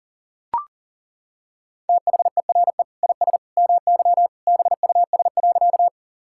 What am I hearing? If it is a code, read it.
Morse code, heard as a keyed tone, about 37 wpm, 690 Hz. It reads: THERE IS MY BUS.
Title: Sentences from Top 1000 Words - Review Part 6 - 40wpm